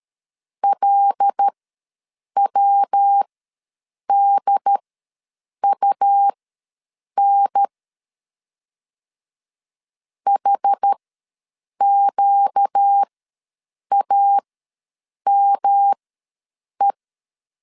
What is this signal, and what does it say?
Morse code, heard as a keyed tone, about 13 wpm, spaced as for 7 wpm, 800 Hz. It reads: LWDUN HQAME
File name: LEZIONI / 5 LBVC / 5 LBV ws 12wpm 3015